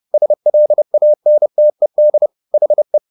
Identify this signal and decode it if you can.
Morse code, heard as a keyed tone, about 30 wpm, 600 Hz. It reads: SLANTED HE